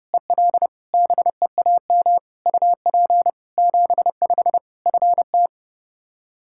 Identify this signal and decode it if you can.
Morse code, heard as a keyed tone, about 30 wpm, 700 Hz. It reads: EL BEAM UP 75 FT
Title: Practice QSO - Exchange 2 - Info - 30wpm